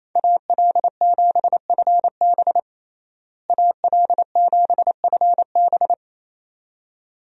Morse code, heard as a keyed tone, about 28 wpm, 700 Hz. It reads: AL7FB AL7FB
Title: USA Call Signs - 5 Characters - Repeated 2x - 28wpm